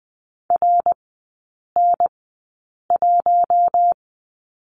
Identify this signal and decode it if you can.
Morse code, heard as a keyed tone, about 20 wpm, spaced as for 8 wpm, 700 Hz. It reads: RN1
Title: 053 Lesson 10e - Letter r - Three Characters 20wpm